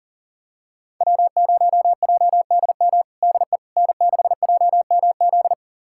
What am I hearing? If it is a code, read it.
Morse code, heard as a keyed tone, about 40 wpm, 700 Hz. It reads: W0JDM DE N6JMZ
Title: USA Callsign DE Callsign 40wpm